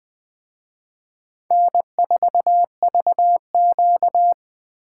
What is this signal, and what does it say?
Morse code, heard as a keyed tone, about 24 wpm, 700 Hz. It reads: N4VQ
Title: Speed Racing - USA 1x2 Call Signs 20wpm